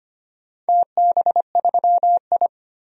Morse code, heard as a keyed tone, about 25 wpm, 700 Hz. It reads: TB3I
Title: DXCC-100 Call Signs - 4 Characters - 25wpm